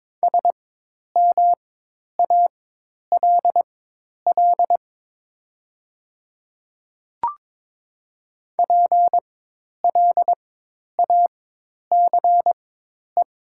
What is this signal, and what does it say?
Morse code, heard as a keyed tone, about 22 wpm, spaced as for 10 wpm, 700 Hz. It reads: SMALL PLACE
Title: 4x Character Spacing - Top 200 Words - 22wpm